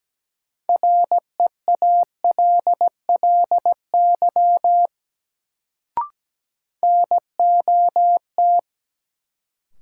Morse code, heard as a keyed tone, about 17 wpm, 700 Hz. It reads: REALLY NOT
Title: Sentences from Top 600 Words - New Word Drill - 17wpm